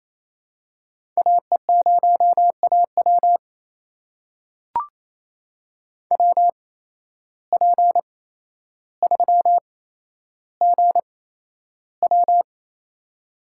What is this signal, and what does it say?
Morse code, heard as a keyed tone, about 28 wpm, spaced as for 12 wpm, 700 Hz. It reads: AE0AW WP3GW
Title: USA Call Signs - 5 Characters - 8x Character Spacing - 28wpm